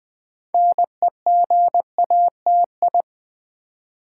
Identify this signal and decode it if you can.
Morse code, heard as a keyed tone, about 20 wpm, 700 Hz. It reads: NEGATI
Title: Build A Word - 3 Letters At A Time - 8-Letter Words - 20wpm